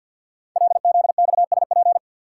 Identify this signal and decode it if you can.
Morse code, heard as a keyed tone, about 50 wpm, 700 Hz. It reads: LZXSP
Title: Code Groups - 5 Letters - 50wpm